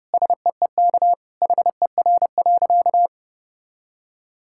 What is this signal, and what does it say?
Morse code, heard as a keyed tone, about 30 wpm, 700 Hz. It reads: SEEK HER.
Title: Sentences from Top 1600 Words 30wpm